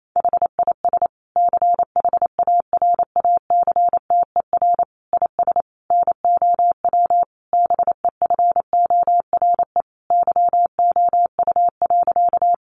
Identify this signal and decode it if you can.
Morse code, heard as a keyed tone, about 28 wpm, 700 Hz. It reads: HIS CHARACTER IS NOW BEFORE YOU.
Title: Sentences from Top 1200 Words 28wpm